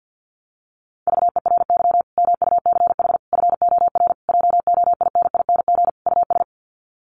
Morse code, heard as a keyed tone, about 48 wpm, 700 Hz. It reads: VERY MUCH FOR JOINING US
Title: Sets of 5 Words 50wpm